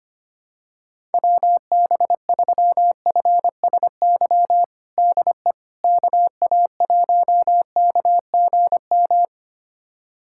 Morse code, heard as a keyed tone, about 25 wpm, 700 Hz. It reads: WB3FSY DE KA1KGM
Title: USA Callsign DE Callsign - 6 Characters - 25wpm